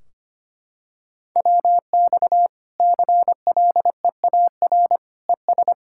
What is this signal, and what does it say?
Morse code, heard as a keyed tone, about 25 wpm, 700 Hz. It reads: WX CLEAR ES
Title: Practice QSO - Exchange 2 - Info - 25wpm